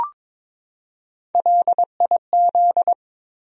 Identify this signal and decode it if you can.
Morse code, heard as a keyed tone, about 22 wpm, 700 Hz. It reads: LIZ
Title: USA Common Names - Max 3 Letters - 22wpm